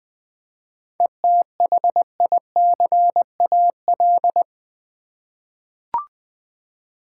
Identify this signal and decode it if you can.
Morse code, heard as a keyed tone, about 20 wpm, 700 Hz. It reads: ETHICAL
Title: 7 Letter Words - First 3 Letters Given - 20wpm